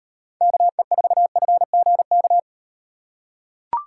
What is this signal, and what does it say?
Morse code, heard as a keyed tone, about 38 wpm, 700 Hz. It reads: KE4FGK